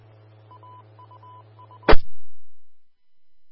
Morse code, heard as a keyed tone, about 20 wpm, 3000 Hz. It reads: AUU
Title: Austin, TX Amateur Repeaters